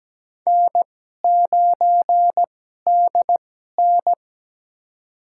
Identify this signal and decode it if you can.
Morse code, heard as a keyed tone, about 17 wpm, spaced as for 12 wpm, 700 Hz. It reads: N9DN
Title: USA Call Signs - 4 Characters - 2x Character Spacing - 17wpm